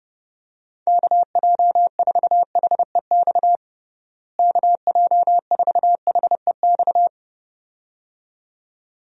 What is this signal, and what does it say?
Morse code, heard as a keyed tone, about 32 wpm, 700 Hz. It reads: KJ4HEX KJ4HEX
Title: USA Call Signs - 6 Characters - Repeated 2x - 30wpm